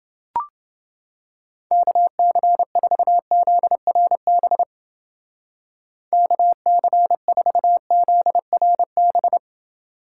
Morse code, heard as a keyed tone, about 31 wpm, 700 Hz. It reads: KC4ZRB KC4ZRB